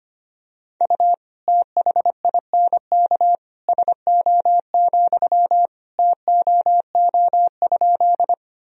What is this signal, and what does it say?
Morse code, heard as a keyed tone, about 25 wpm, 700 Hz. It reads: U THINK SO, TOO?
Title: Sentences from Top 200 Words and Abbreviations 25wpm